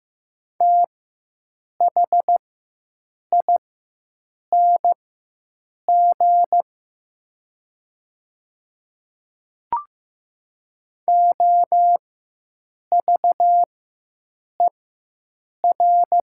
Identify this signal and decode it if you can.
Morse code, heard as a keyed tone, about 15 wpm, spaced as for 7 wpm, 700 Hz. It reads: THING OVER
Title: 4x Character Spacing - Top 100 Words - 15wpm